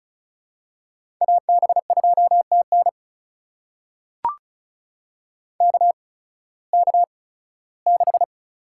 Morse code, heard as a keyed tone, about 36 wpm, spaced as for 19 wpm, 700 Hz. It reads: AB2TN KK6
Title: USA Call Signs - 5 Characters - 8x Character Spacing - 35wpm